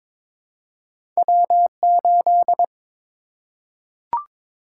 Morse code, heard as a keyed tone, about 22 wpm, 700 Hz. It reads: W8
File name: practice / 22wpm / Two Letter-Number Combinations - 22wpm